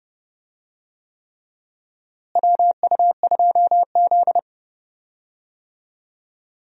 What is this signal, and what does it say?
Morse code, heard as a keyed tone, about 31 wpm, 700 Hz. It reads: WU2Z
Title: USA Call Signs - 4 Characters - Repeated 3x - 30wpm